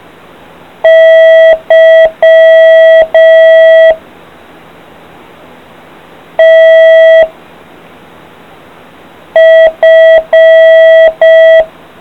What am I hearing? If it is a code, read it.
Morse code, heard as a keyed tone, about 5 wpm, 650 Hz. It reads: Y T F